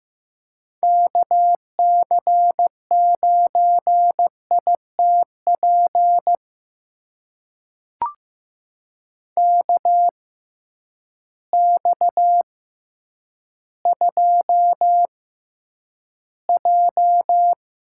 Morse code, heard as a keyed tone, about 15 wpm, spaced as for 9 wpm, 700 Hz. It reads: KC9ITP KX2J
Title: USA Call Signs - 6x Character Spacing - 15wpm